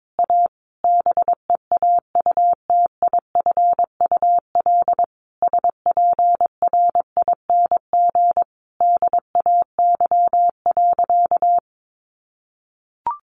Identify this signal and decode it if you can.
Morse code, heard as a keyed tone, about 22 wpm, 700 Hz. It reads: A BEAUTIFUL SPRING DAY.
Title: Sentences from Top 1200 Words 22wpm